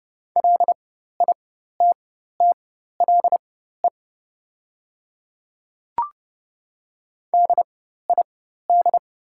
Morse code, heard as a keyed tone, about 30 wpm, spaced as for 14 wpm, 700 Hz. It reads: LITTLE DID
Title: 4x Character Spacing - Top 500 Words - 30wpm